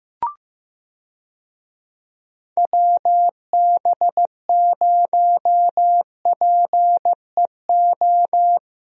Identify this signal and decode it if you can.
Morse code, heard as a keyed tone, about 15 wpm, 700 Hz. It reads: WB0PEO
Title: USA Call Signs - 6 Characters - Repeated 2x - 15wpm